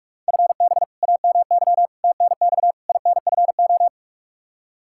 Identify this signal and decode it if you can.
Morse code, heard as a keyed tone, about 41 wpm, 700 Hz. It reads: FB AMY TNX INFO
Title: Practice QSO - Exchange 3 - More Info - 45wpm